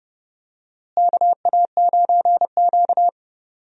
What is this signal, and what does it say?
Morse code, heard as a keyed tone, about 29 wpm, 700 Hz. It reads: KA9Q
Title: USA Call Signs - 4 Characters - Repeated 2x - 30wpm